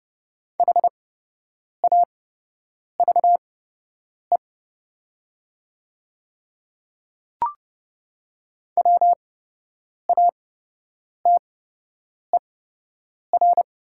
Morse code, heard as a keyed tone, about 30 wpm, spaced as for 8 wpm, 700 Hz. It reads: HAVE WATER
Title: 8x Character Spacing - Top 200 Words - 30wpm